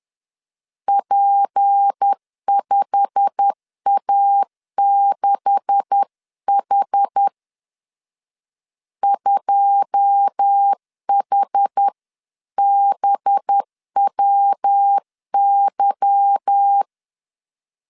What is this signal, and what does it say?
Morse code, heard as a keyed tone, about 10 wpm, 800 Hz. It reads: P5A6H 2HBWY